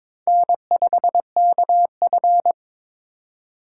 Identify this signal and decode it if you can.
Morse code, heard as a keyed tone, about 22 wpm, 700 Hz. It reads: N5KF
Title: USA Call Signs - 4 Characters - 2x Character Spacing - 22wpm